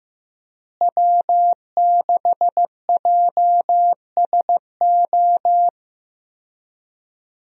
Morse code, heard as a keyed tone, about 15 wpm, 700 Hz. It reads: W6JSO
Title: USA Call Signs - 5 Characters - Repeated 3x - 15wpm